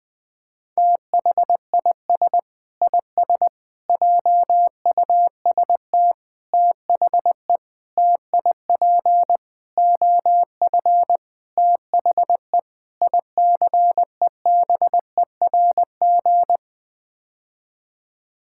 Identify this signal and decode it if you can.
Morse code, heard as a keyed tone, about 20 wpm, 700 Hz. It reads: THIS IS JUST THE TIP OF THE ICEBERG
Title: Sets of 8 Words - 20wpm